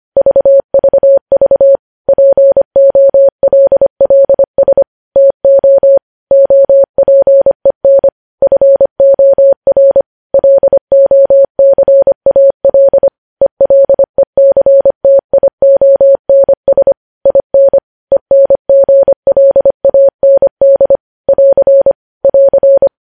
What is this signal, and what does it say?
Morse code, heard as a keyed tone, about 25 wpm, 550 Hz. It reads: VVV POLLS TO OPEN FOR LOCAL ELECTIONS IN ENGLAND <AR> <AR>
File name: podcast / News headlines-1x-25x25